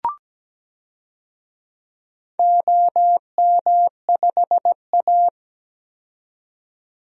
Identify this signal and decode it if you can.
Morse code, heard as a keyed tone, about 17 wpm, 700 Hz. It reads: OM5A
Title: DXCC-100 Call Signs - 4 Characters - 17wpm